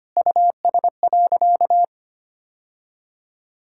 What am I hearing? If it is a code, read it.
Morse code, heard as a keyed tone, about 25 wpm, 700 Hz. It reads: US.
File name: practice / 25wpm / Sentences with 2-Word Phrases - 25wpm